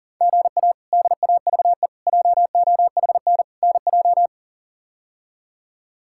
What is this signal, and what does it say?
Morse code, heard as a keyed tone, about 37 wpm, 700 Hz. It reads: GA DAVE JOHN NJ
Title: K1USN SST QSO Practice - 40wpm